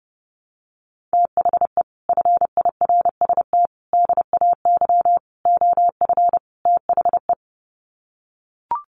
Morse code, heard as a keyed tone, about 30 wpm, 700 Hz. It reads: THE FIRST DAY OF THE
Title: Sets of 5 Words Encore2 30wpm